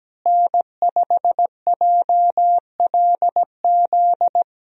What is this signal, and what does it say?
Morse code, heard as a keyed tone, about 17 wpm, 700 Hz. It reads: N5JLZ
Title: USA Call Signs - 5 Characters - Repeated 2x - 17wpm